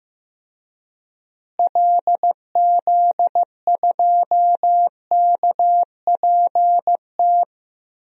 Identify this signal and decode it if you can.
Morse code, heard as a keyed tone, about 15 wpm, 700 Hz. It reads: LZ2KPT
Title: DXCC-100 Call Signs - 6 Characters - Repeated 3x - 15wpm